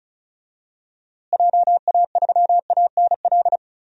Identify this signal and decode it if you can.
Morse code, heard as a keyed tone, about 36 wpm, 700 Hz. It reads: JA3ANL
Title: DXCC-100 Call Signs - 6 Characters - Repeated 2x - 35wpm